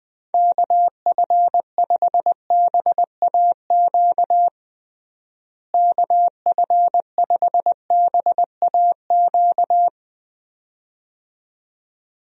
Morse code, heard as a keyed tone, about 20 wpm, 700 Hz. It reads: KF5BAQ KF5BAQ
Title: USA Call Signs - 6 Characters - Repeated 2x - 20wpm